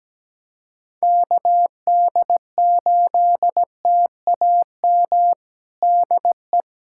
Morse code, heard as a keyed tone, about 17 wpm, 700 Hz. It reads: KD8TAM DE
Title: USA Callsign DE Callsign - 6 Characters - 17wpm